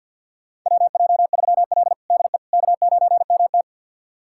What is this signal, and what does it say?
Morse code, heard as a keyed tone, about 52 wpm, 700 Hz. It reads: WJ3L DE K9MT
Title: USA Callsign DE Callsign - 4 Characters - 50wpm